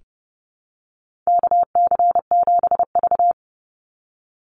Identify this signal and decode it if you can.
Morse code, heard as a keyed tone, about 30 wpm, 700 Hz. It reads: KC7V
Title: Call Signs - 30wpm